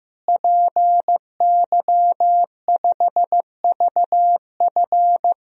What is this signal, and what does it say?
Morse code, heard as a keyed tone, about 15 wpm, 700 Hz. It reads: PY5VF
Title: DXCC-100 Call Signs - 5 Characters - Repeated 3x - 15wpm